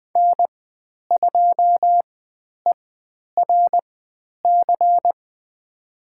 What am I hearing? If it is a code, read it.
Morse code, heard as a keyed tone, about 20 wpm, spaced as for 10 wpm, 700 Hz. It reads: N2ERC
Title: 102 Lesson 17g - Letter u - Call Signs 20wpm